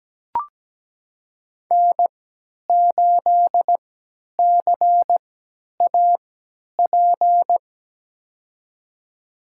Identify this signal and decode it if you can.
Morse code, heard as a keyed tone, about 17 wpm, spaced as for 10 wpm, 700 Hz. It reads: N8CAP
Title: USA Call Signs - 5 Characters - 3x Character Spacing - 17wpm